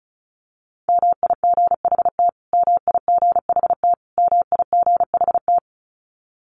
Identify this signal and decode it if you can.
Morse code, heard as a keyed tone, about 34 wpm, 700 Hz. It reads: MIGHT MIGHT MIGHT
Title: Top 500 Words Repeated 35wpm